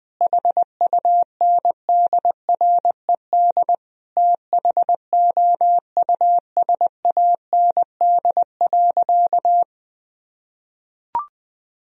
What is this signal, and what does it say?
Morse code, heard as a keyed tone, about 21 wpm, 700 Hz. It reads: HUNDRED THOUSAND.
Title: Sentences from Top 1000 Words - Review Part 3 - 20wpm